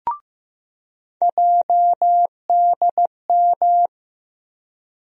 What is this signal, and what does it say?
Morse code, heard as a keyed tone, about 15 wpm, 700 Hz. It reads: JDM
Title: USA Call Sign Postfixes - 3 Letters 15wpm